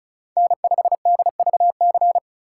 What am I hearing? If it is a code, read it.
Morse code, heard as a keyed tone, about 36 wpm, 700 Hz. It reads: N5DVC
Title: USA Call Signs - 5 Characters - 3x Character Spacing - 35wpm